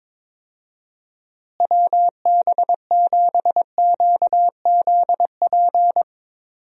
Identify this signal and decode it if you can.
Morse code, heard as a keyed tone, about 22 wpm, 700 Hz. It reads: WB7QZP
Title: USA Call Signs - 6 Characters - Repeated 2x - 22wpm